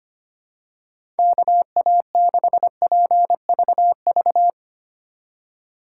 Morse code, heard as a keyed tone, about 25 wpm, 700 Hz. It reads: KA6PVV
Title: USA Call Signs - 3x Character Spacing - 25wpm